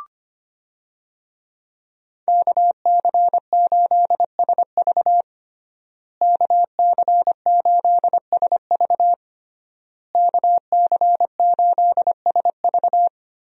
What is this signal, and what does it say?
Morse code, heard as a keyed tone, about 25 wpm, 700 Hz. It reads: KC8SV KC8SV KC8SV
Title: USA Call Signs - 5 Characters - Repeated 3x - 25wpm